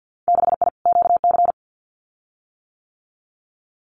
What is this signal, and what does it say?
Morse code, heard as a keyed tone, about 50 wpm, 700 Hz. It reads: 6I QC
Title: Winter Field Day Exchanges Encore 50wpm